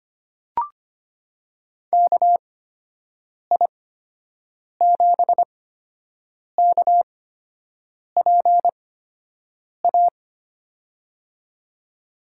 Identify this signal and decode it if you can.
Morse code, heard as a keyed tone, about 25 wpm, spaced as for 7 wpm, 700 Hz. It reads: KI7KPA
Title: USA Call Signs - 8x Character Spacing - 25wpm